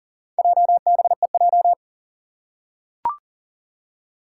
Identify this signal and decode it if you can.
Morse code, heard as a keyed tone, about 40 wpm, 700 Hz. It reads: JBEJ
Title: Four Challenging Letter-Number Combinations 40wpm